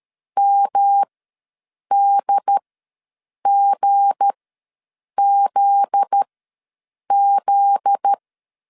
Morse code, heard as a keyed tone, about 13 wpm, spaced as for 7 wpm, 800 Hz. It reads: MDGZZ
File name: 7 GF9Z ws 12wpm